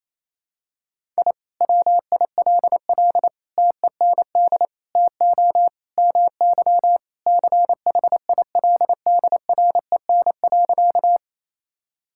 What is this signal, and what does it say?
Morse code, heard as a keyed tone, about 28 wpm, 700 Hz. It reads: I WILL TEND TO MY CHILDREN.
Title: Sentences from Top 1000 Words - Review Part 5 - 28wpm